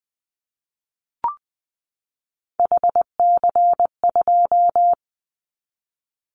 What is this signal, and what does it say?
Morse code, heard as a keyed tone, about 20 wpm, 700 Hz. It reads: HC2
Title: Callsign Prefixes for Top 100 DXCC - 20wpm